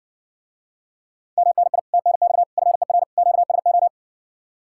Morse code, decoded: MNI TNX FER QSO

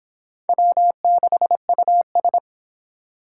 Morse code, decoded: W6US

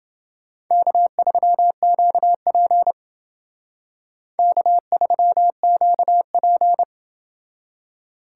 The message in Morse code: K3QP K3QP

